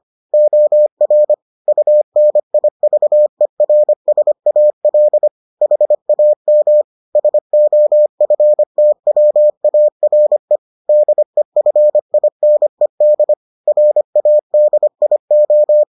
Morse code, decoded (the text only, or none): OR UNIVERSAL HAM SOFTWARE DEFINED RADIO